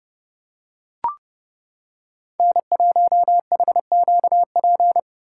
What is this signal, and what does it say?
Morse code, heard as a keyed tone, about 29 wpm, 700 Hz. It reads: N1HQP